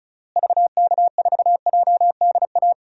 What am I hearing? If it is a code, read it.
Morse code, heard as a keyed tone, about 36 wpm, 700 Hz. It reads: VK4JDA
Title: DXCC-100 Call Signs - 6 Characters - Repeated 3x - 35wpm